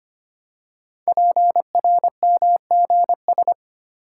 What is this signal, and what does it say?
Morse code, heard as a keyed tone, about 25 wpm, 700 Hz. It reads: PRMGS
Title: Code Groups - 5 Letters - 25wpm